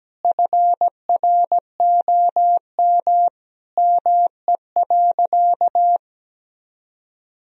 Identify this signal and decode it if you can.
Morse code, heard as a keyed tone, about 17 wpm, 700 Hz. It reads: FROM ME.
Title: Sentences from Top 1000 Words - Review Part 1 - 17wpm